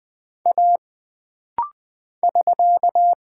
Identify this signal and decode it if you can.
Morse code, heard as a keyed tone, about 20 wpm, 700 Hz. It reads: A <SK>
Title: Single Character (Letter:Number:Punctuation:Prosign) - Rapid-Fire - 20wpm